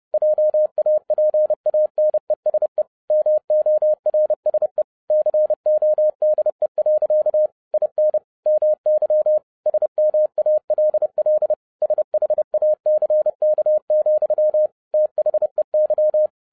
Morse code, heard as a keyed tone, about 30 wpm, 610 Hz. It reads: JAPANESE MORSE CODE. IN MY SMALL SHACK, THEY